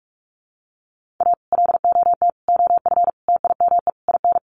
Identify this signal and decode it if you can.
Morse code, heard as a keyed tone, about 44 wpm, 700 Hz. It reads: A LOT OF TIME IN
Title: Sets of 5 Words Encore2 45wpm